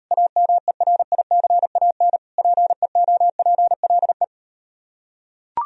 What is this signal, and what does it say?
Morse code, heard as a keyed tone, about 38 wpm, 700 Hz. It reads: AMERICAN PEOPLE